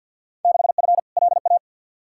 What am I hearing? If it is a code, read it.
Morse code, heard as a keyed tone, about 48 wpm, 700 Hz. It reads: 6F LA
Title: Field Day Exchanges Encore - 50wpm